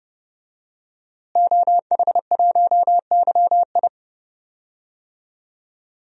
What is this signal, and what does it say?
Morse code, heard as a keyed tone, about 30 wpm, 700 Hz. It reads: OH1YI
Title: DXCC-100 Call Signs - 5 Characters - Repeated 2x - 30wpm